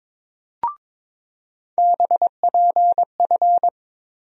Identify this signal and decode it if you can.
Morse code, heard as a keyed tone, about 22 wpm, 700 Hz. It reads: BPF
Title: USA Call Sign Postfixes - 3 Letters 22wpm